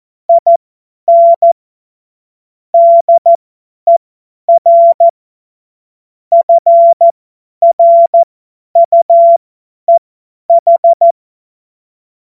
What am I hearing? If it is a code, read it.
Morse code, heard as a keyed tone, about 14 wpm, spaced as for 10 wpm, 680 Hz. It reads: IN DER FRUEH
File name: Morsekurs / Witze / Witz 26 März 2024 50 F70